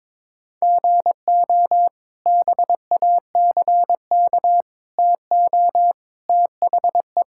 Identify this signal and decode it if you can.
Morse code, heard as a keyed tone, about 22 wpm, 700 Hz. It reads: GO BACK TO THE
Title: Sentences from Top 1000 Words - Review Part 5 - 22wpm